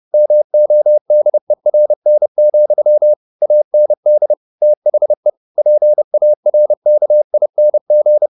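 Morse code, heard as a keyed tone, about 30 wpm, 600 Hz. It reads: MODERN, AND THE PARKING
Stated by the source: Morse code at 29 wpm, sending Ken Follett's novel Never in English, from CW med Gnister.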